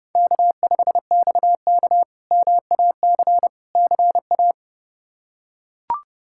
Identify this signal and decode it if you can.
Morse code, heard as a keyed tone, about 32 wpm, 700 Hz. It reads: K5XK MAC CA
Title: K1USN SST QSO Practice - 30wpm